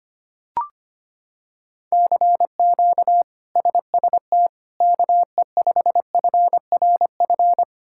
Morse code, heard as a keyed tone, about 25 wpm, 700 Hz. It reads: CQ SST KE5FRF
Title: K1USN SST QSO Practice - 25wpm